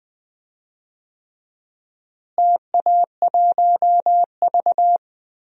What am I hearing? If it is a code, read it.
Morse code, heard as a keyed tone, about 20 wpm, 700 Hz. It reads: TA1V
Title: DXCC-100 Call Signs - 4 Characters - Repeated 3x - 20wpm